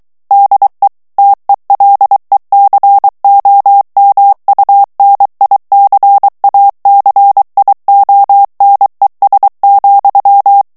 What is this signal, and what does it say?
Morse code, heard as a keyed tone, about 23 wpm, 800 Hz. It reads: DE TELECOMUNICACIONES,
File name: morse mp3 / ESPAGNOL TEXTE EN CLAIR 2 VITESSE 22 MOTS MINUTES